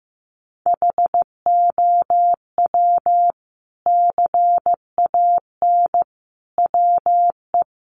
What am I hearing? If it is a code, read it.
Morse code, heard as a keyed tone, about 15 wpm, 700 Hz. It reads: HOW CAN WE
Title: Sentences from Top 1200 Words 15wpm